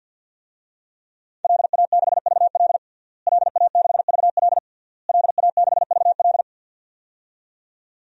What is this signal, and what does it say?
Morse code, heard as a keyed tone, about 47 wpm, 700 Hz. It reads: LA6VL LA6VL LA6VL
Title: DXCC-100 Call Signs - 5 Characters - Repeated 3x - 50wpm